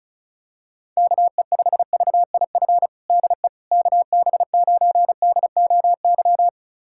Morse code, heard as a keyed tone, about 36 wpm, 700 Hz. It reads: KE5VIF DE KB9DOY
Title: USA Callsign DE Callsign - 6 Characters - 35wpm